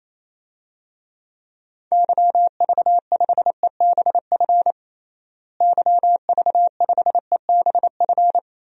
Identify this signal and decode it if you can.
Morse code, heard as a keyed tone, about 28 wpm, 700 Hz. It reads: YV5EBF YV5EBF
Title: DXCC-100 Call Signs - 6 Characters - Repeated 2x - 28wpm